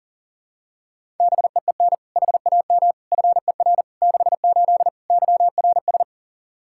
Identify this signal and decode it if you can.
Morse code, heard as a keyed tone, about 40 wpm, 700 Hz. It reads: BEEN HAM FER 68 YRS
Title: K1USN SST QSO Practice 40wpm